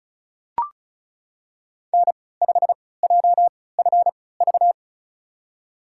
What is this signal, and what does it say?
Morse code, heard as a keyed tone, about 35 wpm, spaced as for 20 wpm, 700 Hz. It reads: N5JFV
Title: USA Call Signs - 5 Characters - 3x Character Spacing - 35wpm